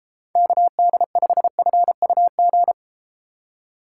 Morse code, heard as a keyed tone, about 33 wpm, 700 Hz. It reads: KD5FUG